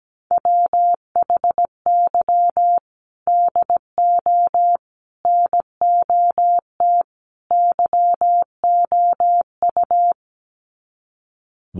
Morse code, heard as a keyed tone, about 17 wpm, 700 Hz. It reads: WHY DO NOT YOU
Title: Sets of 4 Words 17wpm